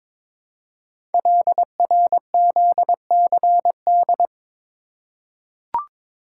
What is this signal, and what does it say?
Morse code, heard as a keyed tone, about 22 wpm, 700 Hz. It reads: LRZCD
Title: Code Groups - 5 Letters - 22wpm